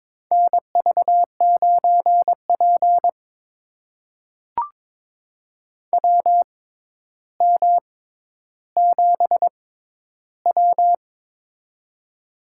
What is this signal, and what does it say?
Morse code, heard as a keyed tone, about 22 wpm, spaced as for 11 wpm, 700 Hz. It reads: NV9P WM7W
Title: USA Call Signs - 4 Characters - 6x Character Spacing - 22wpm